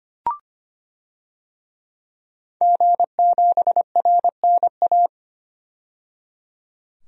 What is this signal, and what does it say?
Morse code, heard as a keyed tone, about 25 wpm, 700 Hz. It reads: G7RNA
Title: DXCC-100 Call Signs - 5 Characters - 25wpm